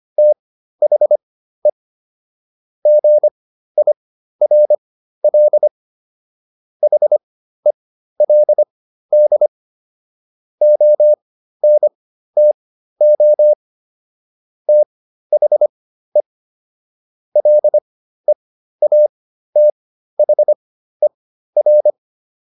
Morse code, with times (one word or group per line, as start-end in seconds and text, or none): 0.2-1.7 THE
2.9-5.7 GIRL
6.8-9.5 HELD
10.6-13.5 ONTO
14.7-16.2 THE
17.4-21.9 LEATHER